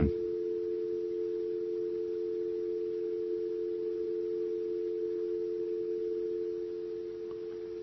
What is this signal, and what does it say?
Dial tone (North America), 350 + 440 Hz.